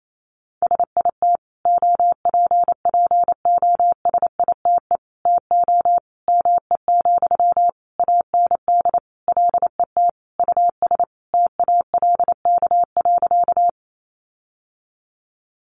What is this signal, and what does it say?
Morse code, heard as a keyed tone, about 28 wpm, 700 Hz. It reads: SIT OPPOSITE TO ME, AND LET US TALK.
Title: Sentences from Top 1300 Words 28wpm